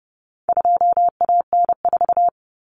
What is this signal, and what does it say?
Morse code, heard as a keyed tone, about 30 wpm, 700 Hz. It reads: 2AN4
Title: Common Cut Numbers - 4-Digit 30wpm